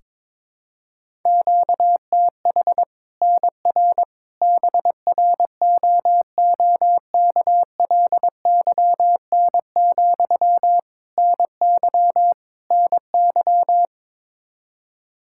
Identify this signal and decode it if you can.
Morse code, heard as a keyed tone, about 22 wpm, 700 Hz. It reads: QTH NR BROOKLYN, NY NY
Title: Practice QSO - Exchange 1 - Beginning - 22wpm